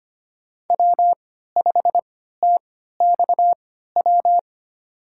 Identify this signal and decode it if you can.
Morse code, heard as a keyed tone, about 25 wpm, spaced as for 14 wpm, 700 Hz. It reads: W5TXW
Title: USA Call Signs - 5 Characters - 3x Character Spacing - 25wpm